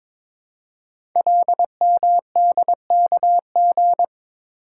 Morse code, heard as a keyed tone, about 22 wpm, 700 Hz. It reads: LMDKG
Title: Code Groups - 5 Letters - 22wpm